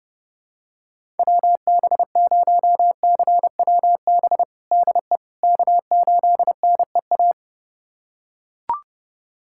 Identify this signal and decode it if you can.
Morse code, heard as a keyed tone, about 30 wpm, 700 Hz. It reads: WB0CWB DE K8NEA
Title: USA Callsign DE Callsign 30wpm